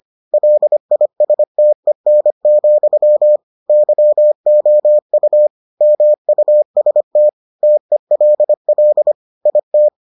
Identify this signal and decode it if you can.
Morse code, heard as a keyed tone, about 25 wpm, 600 Hz. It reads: LISTEN, YOU MUST TELL IT